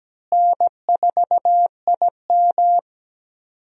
Morse code, heard as a keyed tone, about 17 wpm, 700 Hz. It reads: N4IM
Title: USA Call Signs - 4 Characters - Repeated 2x - 17wpm